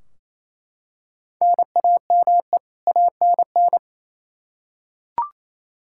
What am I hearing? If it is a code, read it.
Morse code, heard as a keyed tone, about 28 wpm, 700 Hz. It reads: NAME ANN
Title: K1USN SST QSO Practice 28wpm